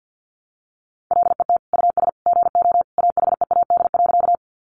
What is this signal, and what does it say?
Morse code, heard as a keyed tone, about 50 wpm, 700 Hz. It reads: LET US GO AHEAD.
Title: Sentences from Top 1100 Words 50wpm